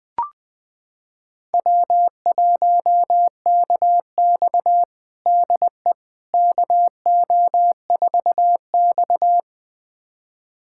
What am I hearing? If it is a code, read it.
Morse code, heard as a keyed tone, about 20 wpm, 700 Hz. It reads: W1KX DE KO4X
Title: USA Callsign DE Callsign - 4 Characters - 20wpm